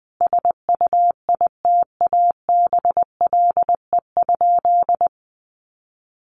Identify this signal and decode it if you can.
Morse code, heard as a keyed tone, about 20 wpm, 700 Hz. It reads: SUITABLE?
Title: Sentences from Top 1400 Words 20wpm